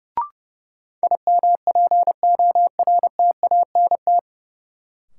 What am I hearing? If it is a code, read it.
Morse code, heard as a keyed tone, about 30 wpm, 700 Hz. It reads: IMPORTANT
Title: Sets of 3 Words - New Word Drill - 30wpm